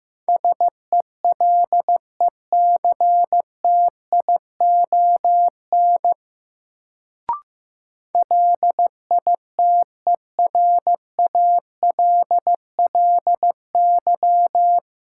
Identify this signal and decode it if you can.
Morse code, heard as a keyed tone, about 15 wpm, 700 Hz. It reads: SELECTION LITERALLY
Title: Sentences from Top 1400 Words - New Word Drill - 15wpm